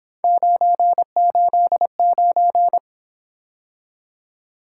Morse code, heard as a keyed tone, about 26 wpm, 700 Hz. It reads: 989
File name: Speed-Racing - Three Letter-Number Combinations - 22wpm